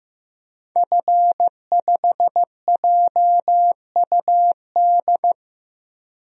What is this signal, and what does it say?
Morse code, heard as a keyed tone, about 15 wpm, 700 Hz. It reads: F5JUD
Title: DXCC-100 Call Signs - 5 Characters - 15wpm